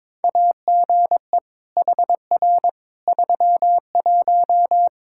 Morse code, heard as a keyed tone, about 22 wpm, 700 Hz. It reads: AGE HR 31
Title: Practice QSO - Exchange 3 - More Info - 22wpm